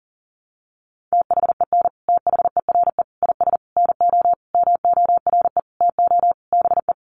Morse code, heard as a keyed tone, about 40 wpm, 700 Hz. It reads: THEN THERE IS NO MORE TO BE
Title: Sentences from Top 700 Words 40wpm